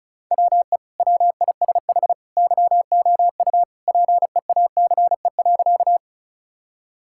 Morse code, heard as a keyed tone, about 36 wpm, 700 Hz. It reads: WE WISH YOU PEACE.
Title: Sentences from Top 1600 Words 35wpm